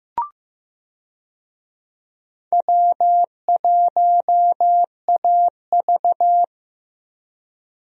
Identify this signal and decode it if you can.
Morse code, heard as a keyed tone, about 16 wpm, 700 Hz. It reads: W1AV
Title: USA Call Signs - 4 Characters - Repeated 2x - 15wpm